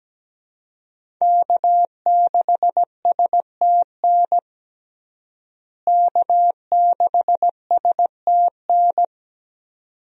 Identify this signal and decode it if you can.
Morse code, heard as a keyed tone, about 17 wpm, 700 Hz. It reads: K6STN K6STN
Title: USA Call Signs - 5 Characters - Repeated 2x - 17wpm